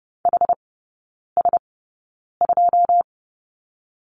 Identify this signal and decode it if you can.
Morse code, heard as a keyed tone, about 30 wpm, spaced as for 9 wpm, 700 Hz. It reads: HS2
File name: 081 Lesson 14e - Number 2 - Three Characters 30wpm